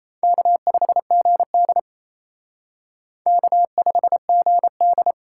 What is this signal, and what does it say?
Morse code, heard as a keyed tone, about 34 wpm, 700 Hz. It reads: K5GD K5GD
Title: Speed Racing - USA 1x2 Call Signs 28wpm